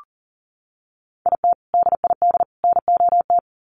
Morse code, heard as a keyed tone, about 40 wpm, 700 Hz. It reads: IT DID NOT
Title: Sets of 3 Words 40wpm